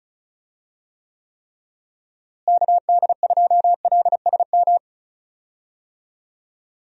Morse code, heard as a keyed tone, about 36 wpm, 700 Hz. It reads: KD2LSM
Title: USA Call Signs - 6 Characters - Repeated 3x - 35wpm